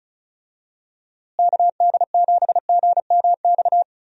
Morse code, heard as a keyed tone, about 36 wpm, 700 Hz. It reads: KD7GMX
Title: USA Call Signs - 6 Characters - Repeated 3x - 35wpm